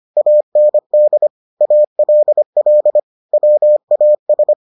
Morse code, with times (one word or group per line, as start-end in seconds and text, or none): 0.2-1.3 AND
1.6-3.0 ALL
3.3-4.5 WAS